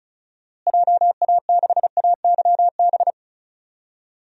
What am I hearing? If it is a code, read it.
Morse code, heard as a keyed tone, about 36 wpm, 700 Hz. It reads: JA6AYB
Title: DXCC-100 Call Signs - 6 Characters - Repeated 2x - 35wpm